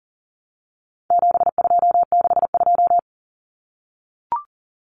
Morse code, heard as a keyed tone, about 40 wpm, 700 Hz. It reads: 7262
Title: Common Cut Numbers - 4-Digit 40wpm